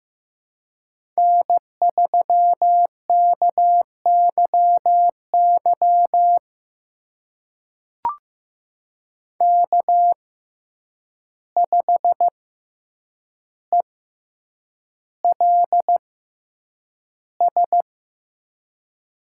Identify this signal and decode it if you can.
Morse code, heard as a keyed tone, about 15 wpm, spaced as for 8 wpm, 700 Hz. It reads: N3KYY K5ELS
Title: USA Call Signs - 5 Characters - 6x Character Spacing - 15wpm